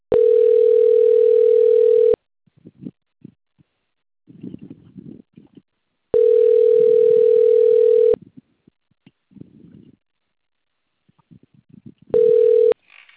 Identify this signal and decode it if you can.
Ringback tone (North America), 440 + 480 Hz.